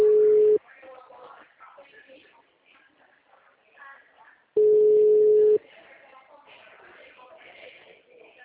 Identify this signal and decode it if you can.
Ringback tone (Europe, 425 Hz), with speech.